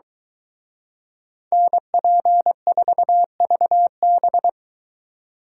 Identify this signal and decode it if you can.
Morse code, heard as a keyed tone, about 23 wpm, 700 Hz. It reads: NP4VB